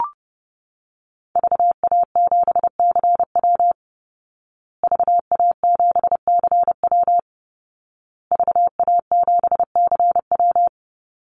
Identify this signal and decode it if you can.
Morse code, heard as a keyed tone, about 30 wpm, 700 Hz. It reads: VA7CW VA7CW VA7CW
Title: Call Signs Repeated 2x - 30wpm